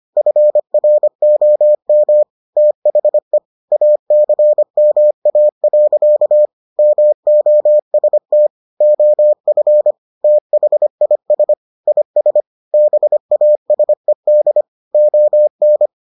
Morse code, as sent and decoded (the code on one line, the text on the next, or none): ..-. .-. --- -- / - .... / .- -.-. -- .- .-.-.- / -- --- ... - / --- ..-. / - .... .. ... / .. ... / -... .- ... -.. / --- -.
FROM THE ACMA. MOST OF THIS IS BASED ON